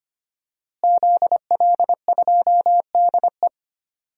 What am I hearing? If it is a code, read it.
Morse code, heard as a keyed tone, about 25 wpm, 700 Hz. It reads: ZL2DE
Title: DXCC-100 Call Signs - 5 Characters - Repeated 3x - 25wpm